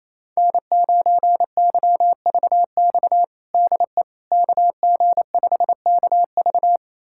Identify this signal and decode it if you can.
Morse code, heard as a keyed tone, about 28 wpm, 700 Hz. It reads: N9YVX DE KG5KV
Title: USA Callsign DE Callsign - 5 Characters - 28wpm